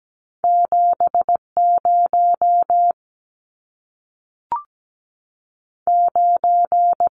Morse code, heard as a keyed tone, about 17 wpm, 700 Hz. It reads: 7 0 9
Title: Numbers - 2-Digit 17wpm